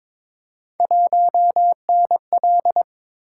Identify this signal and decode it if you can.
Morse code, heard as a keyed tone, about 22 wpm, 700 Hz. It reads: 1NL
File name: practice / 22wpm / Three Letter-Number Combinations - 22wpm